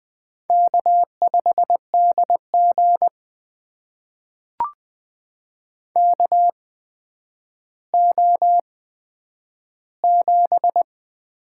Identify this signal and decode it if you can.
Morse code, heard as a keyed tone, about 20 wpm, spaced as for 10 wpm, 700 Hz. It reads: K5DG KO7
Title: USA Call Signs - 4 Characters - 8x Character Spacing - 20wpm